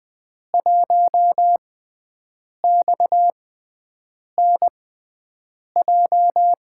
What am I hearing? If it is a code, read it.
Morse code, heard as a keyed tone, about 20 wpm, spaced as for 7 wpm, 700 Hz. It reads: 1XNJ
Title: USA Call Signs - 5 Characters - 6x Character Spacing - 20wpm